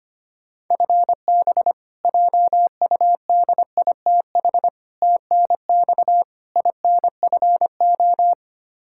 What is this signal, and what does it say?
Morse code, heard as a keyed tone, about 25 wpm, 700 Hz. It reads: FB JUDITH TNX INFO
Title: K1USN SST QSO Practice 25wpm